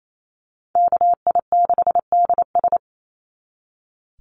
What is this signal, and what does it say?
Morse code, heard as a keyed tone, about 28 wpm, 700 Hz. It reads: KI6DS
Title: NA SOTA Chaser Callsigns 28wpm